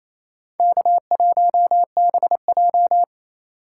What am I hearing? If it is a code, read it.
Morse code, heard as a keyed tone, about 28 wpm, 700 Hz. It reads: K1BJ
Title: USA Call Signs - 4 Characters - 2x Character Spacing - 28wpm